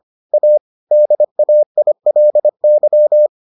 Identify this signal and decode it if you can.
Morse code, heard as a keyed tone, about 25 wpm, 600 Hz. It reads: A DAILY